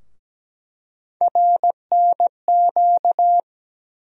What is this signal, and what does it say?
Morse code, heard as a keyed tone, about 17 wpm, 700 Hz. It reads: RNQ
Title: USA Call Sign Postfixes - 3 Letters 17wpm